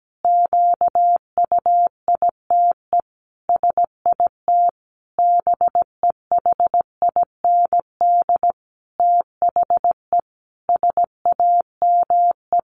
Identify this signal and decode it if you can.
Morse code, heard as a keyed tone, about 17 wpm, 700 Hz. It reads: QUITE SIT BEHIND THE SAME
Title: A Sound Of Thunder 17wpm